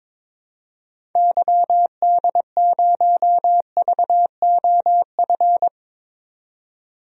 Morse code, heard as a keyed tone, about 22 wpm, 700 Hz. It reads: YD0VOF
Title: DXCC-100 Call Signs - 6 Characters - 22wpm